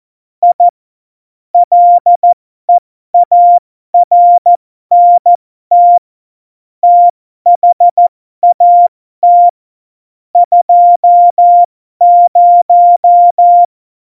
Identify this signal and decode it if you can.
Morse code, heard as a keyed tone, about 14 wpm, spaced as for 12 wpm, 700 Hz. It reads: I LEARNT THAT 20
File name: Morsekurs / Witze